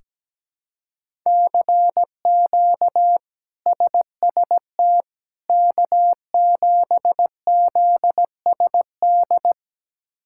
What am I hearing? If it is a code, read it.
Morse code, heard as a keyed tone, about 17 wpm, 700 Hz. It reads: CQ SST K7ZSD
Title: K1USN SST QSO Practice - 17wpm